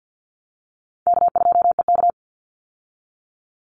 Morse code, heard as a keyed tone, about 50 wpm, 700 Hz. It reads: K2EK
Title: Easy Call Signs Repeated 2x - 50wpm